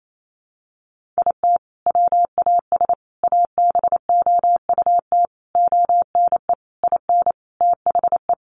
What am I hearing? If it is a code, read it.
Morse code, heard as a keyed tone, about 28 wpm, 700 Hz. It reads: IT WAS ABOUT ONE IN THE
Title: Sentences from Top 1500 Words 28wpm